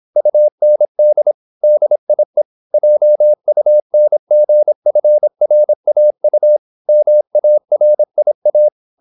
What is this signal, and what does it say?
Morse code, heard as a keyed tone, about 26 wpm, 600 Hz. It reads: UND DIE JUNGFRAU MARIA